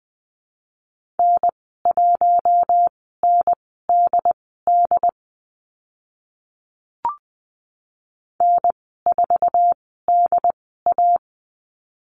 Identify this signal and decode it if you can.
Morse code, heard as a keyed tone, about 20 wpm, spaced as for 15 wpm, 700 Hz. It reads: N1NDD N4DA
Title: 069 Lesson 12g - Letter d - Call Signs 20wpm